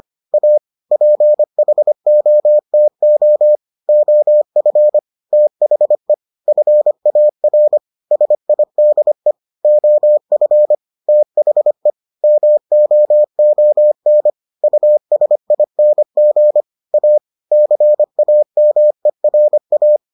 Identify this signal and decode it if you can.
Morse code, heard as a keyed tone, about 25 wpm, 600 Hz. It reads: A PHOTO OF THE FAR SIDE OF THE MOON USING A CAMERA